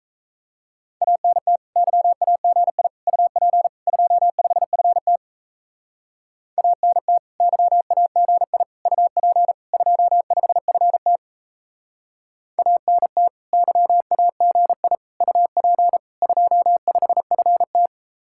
Morse code, heard as a keyed tone, about 37 wpm, 700 Hz. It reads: ANT YAGI UP 25FT ANT YAGI UP 25FT ANT YAGI UP 25FT